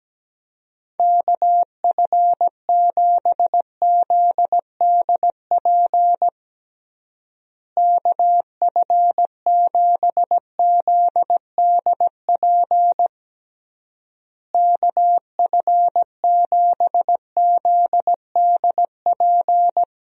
Morse code, heard as a keyed tone, about 17 wpm, 700 Hz. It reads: KF7ZDP KF7ZDP KF7ZDP
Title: USA Call Signs - 6 Characters - Repeated 3x - 17wpm